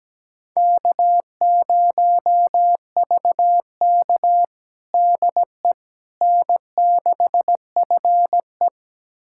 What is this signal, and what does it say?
Morse code, heard as a keyed tone, about 17 wpm, 700 Hz. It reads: K0VK DE N6FE
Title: USA Callsign DE Callsign - 4 Characters - 17wpm